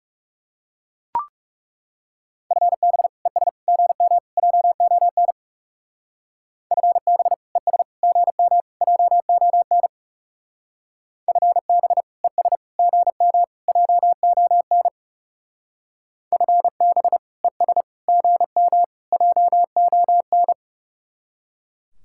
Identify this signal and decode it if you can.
Morse code, heard as a keyed tone, about 34 wpm, 700 Hz. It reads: FB ES GM JON FB ES GM JON FB ES GM JON FB ES GM JON
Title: Speed-Racing Common QSO Phrases - 30wpm